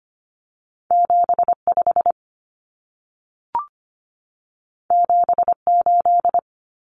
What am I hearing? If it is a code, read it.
Morse code, heard as a keyed tone, about 25 wpm, 700 Hz. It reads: 75 78
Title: Common Cut Numbers - 2-Digit 25wpm